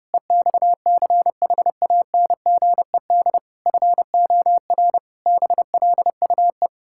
Morse code, heard as a keyed tone, about 30 wpm, 700 Hz. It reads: EXCHANGED FOR BLUE
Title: Pangram Challenge v2 - 30wpm